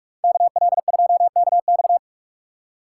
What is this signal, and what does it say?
Morse code, heard as a keyed tone, about 43 wpm, 700 Hz. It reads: KL2KX